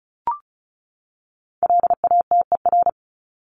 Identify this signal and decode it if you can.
Morse code, heard as a keyed tone, about 34 wpm, 700 Hz. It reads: LATER
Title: Top 501-750 Words 35wpm